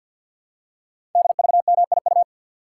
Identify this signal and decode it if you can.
Morse code, heard as a keyed tone, about 48 wpm, 700 Hz. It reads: DVMIU
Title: Code Groups - 5 Letters - 50wpm